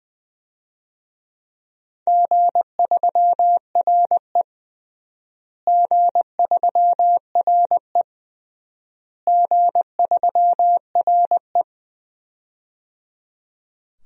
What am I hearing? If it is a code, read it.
Morse code, heard as a keyed tone, about 20 wpm, 700 Hz. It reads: G3RE G3RE G3RE
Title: DXCC-100 Call Signs - 4 Characters - Repeated 3x - 20wpm